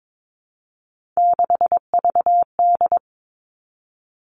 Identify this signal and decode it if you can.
Morse code, heard as a keyed tone, about 22 wpm, 700 Hz. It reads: 6VD
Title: All Cut Numbers - 3-Digit 22wpm